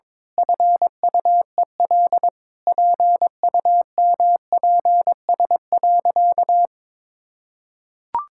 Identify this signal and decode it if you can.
Morse code, heard as a keyed tone, about 22 wpm, 700 Hz. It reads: FUEL PUMPS.
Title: Pangram Challenge v2 - 22wpm